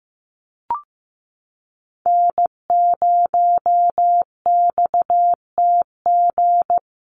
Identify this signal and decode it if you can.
Morse code, heard as a keyed tone, about 15 wpm, 700 Hz. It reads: N0XTG
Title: USA Call Signs Repeated 3x 15wpm